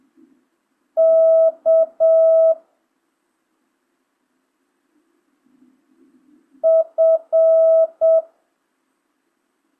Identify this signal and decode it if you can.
Morse code, heard as a keyed tone, about 7 wpm, 650 Hz. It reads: K F
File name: jh8cbh / 2025cw